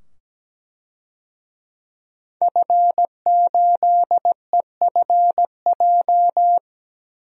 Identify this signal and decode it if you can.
Morse code, heard as a keyed tone, about 17 wpm, 700 Hz. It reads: F8EFJ
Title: DXCC-100 Call Signs - 5 Characters - 17wpm